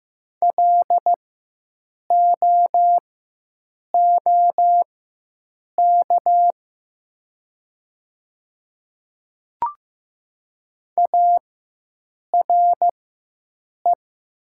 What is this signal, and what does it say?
Morse code, heard as a keyed tone, about 15 wpm, spaced as for 7 wpm, 700 Hz. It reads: LOOK ARE
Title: 4x Character Spacing - Top 100 Words - 15wpm